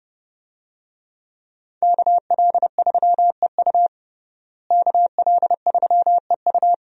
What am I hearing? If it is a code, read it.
Morse code, heard as a keyed tone, about 30 wpm, 700 Hz. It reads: KL3EU KL3EU
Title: USA Call Signs - 5 Characters - Repeated 3x - 30wpm